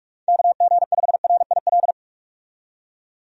Morse code, heard as a keyed tone, about 46 wpm, 700 Hz. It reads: KG5RIL
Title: USA Call Signs - 6 Characters - 45wpm